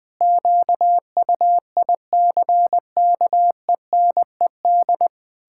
Morse code, heard as a keyed tone, about 20 wpm, 700 Hz. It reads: QUICKENED
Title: Pangram Challenge v2 - 20wpm